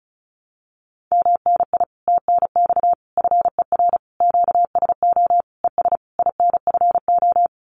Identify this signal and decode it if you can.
Morse code, heard as a keyed tone, about 34 wpm, 700 Hz. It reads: MNI TNX FER QSO ES INFO
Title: Common QSO Phrases Encore 35wpm